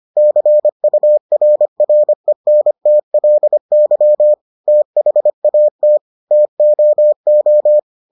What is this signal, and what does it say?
Morse code, heard as a keyed tone, about 25 wpm, 600 Hz. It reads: CURRENTLY THAT TOO